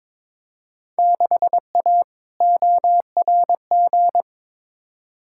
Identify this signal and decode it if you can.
Morse code, heard as a keyed tone, about 22 wpm, 700 Hz. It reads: 6A ORG
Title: Field Day Exchanges Encore - 22wpm